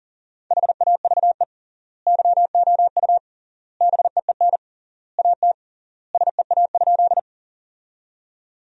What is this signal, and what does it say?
Morse code, heard as a keyed tone, about 40 wpm, 700 Hz. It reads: HAVE YOU BEEN AT SEA?
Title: 3x Word-Spacing - Sentences from Top 300 Words - 40wpm